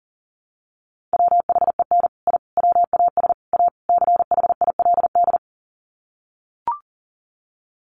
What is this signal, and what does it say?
Morse code, heard as a keyed tone, about 40 wpm, 700 Hz. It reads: WHEN I WAS A CHILD
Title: Sets of 5 Words Encore 40wpm